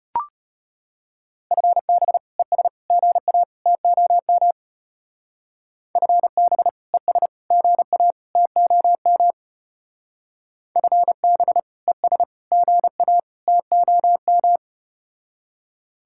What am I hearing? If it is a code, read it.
Morse code, heard as a keyed tone, about 34 wpm, 700 Hz. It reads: FB ES GA TOM FB ES GA TOM FB ES GA TOM